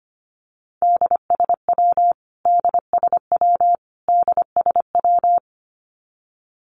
Morse code, heard as a keyed tone, about 25 wpm, 700 Hz. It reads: DSW DSW DSW
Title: Abbreviations Repeated 3x - 25wpm